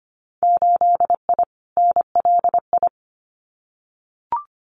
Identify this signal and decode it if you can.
Morse code, heard as a keyed tone, about 25 wpm, 700 Hz. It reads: 8I NLI
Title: Winter Field Day Exchanges Encore 25wpm